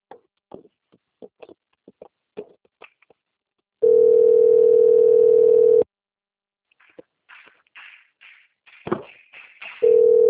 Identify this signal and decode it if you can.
Ringback tone (North America), 440 + 480 Hz.
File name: out- - - - - .13241